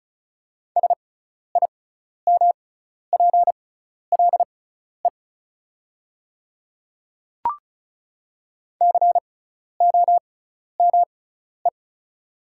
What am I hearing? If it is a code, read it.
Morse code, heard as a keyed tone, about 36 wpm, spaced as for 12 wpm, 700 Hz. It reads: SIMPLE COME
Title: 6x Character Spacing - Top 500 Words - 35wpm